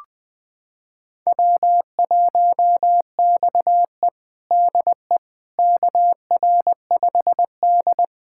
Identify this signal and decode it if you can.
Morse code, heard as a keyed tone, about 20 wpm, 700 Hz. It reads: W1XE DE KR5D
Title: USA Callsign DE Callsign - 4 Characters - 20wpm